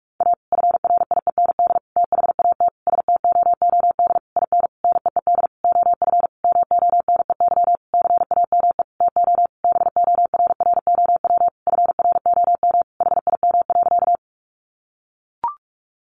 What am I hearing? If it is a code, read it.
Morse code, heard as a keyed tone, about 44 wpm, 700 Hz. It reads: A FRIEND THAT STOOD IN NEED OF MONEY CAME TO BORROW FROM HIM.
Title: Sentences from Top 1500 Words 45wpm